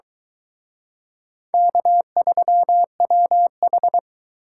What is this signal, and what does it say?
Morse code, heard as a keyed tone, about 23 wpm, 700 Hz. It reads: K3WH